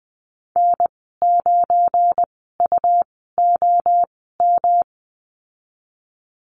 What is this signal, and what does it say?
Morse code, heard as a keyed tone, about 20 wpm, spaced as for 14 wpm, 700 Hz. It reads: N9UOM
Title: 173 Lesson 28g - Number 9 - Call Signs 20wpm